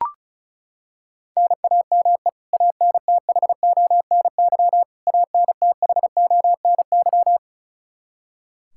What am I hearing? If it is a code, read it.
Morse code, heard as a keyed tone, about 36 wpm, 700 Hz. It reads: NAME ANTHONY ANTHONY
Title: Practice QSO - Exchange 1 - Beginning - 35wpm